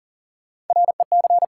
Morse code, heard as a keyed tone, about 40 wpm, 700 Hz. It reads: REC
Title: Sentences from Top 1200 Words - New Word Drill - 40wpm